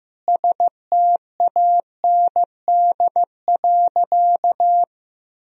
Morse code, heard as a keyed tone, about 15 wpm, 700 Hz. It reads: STAND.
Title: Sentences from Top 1000 Words - Review Part 1 - 15wpm